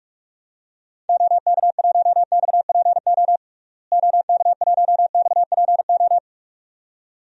Morse code, heard as a keyed tone, about 46 wpm, 700 Hz. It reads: OK1XPO OK1XPO
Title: DXCC-100 Call Signs - 6 Characters - Repeated 2x - 45wpm